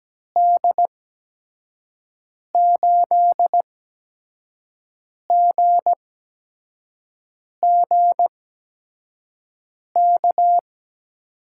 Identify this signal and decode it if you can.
Morse code, heard as a keyed tone, about 17 wpm, 700 Hz. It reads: D 8 G G K
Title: USA Call Signs - 6 Characters - 8x Character Spacing - 17wpm